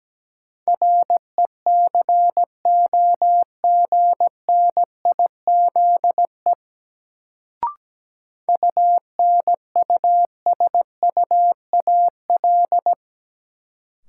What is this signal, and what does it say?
Morse code, heard as a keyed tone, about 17 wpm, 700 Hz. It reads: RECOGNIZE UNUSUAL
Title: Sentences from Top 1400 Words - New Word Drill - 17wpm